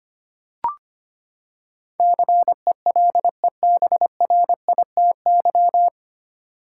Morse code, heard as a keyed tone, about 25 wpm, 700 Hz. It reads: CELEBRITY
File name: Nine Letter Words - 25wpm